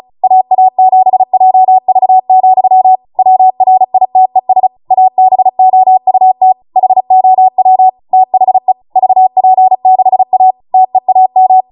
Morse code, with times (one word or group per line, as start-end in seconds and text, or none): 0.2-3.0 AA7JV,
3.2-4.7 WRITES
4.9-6.5 ABOUT
6.8-7.9 HOW
8.1-8.7 THE
9.0-10.5 VP6A
10.7-11.6 TEAM